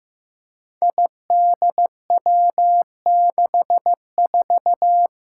Morse code, heard as a keyed tone, about 15 wpm, 700 Hz. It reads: IDW64
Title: Five Challenging Letter-Number Combinations 15wpm